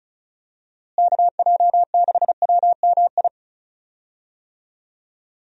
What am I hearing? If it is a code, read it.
Morse code, heard as a keyed tone, about 36 wpm, 700 Hz. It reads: KJ6WMI
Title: USA Call Signs - 6 Characters - Repeated 3x - 35wpm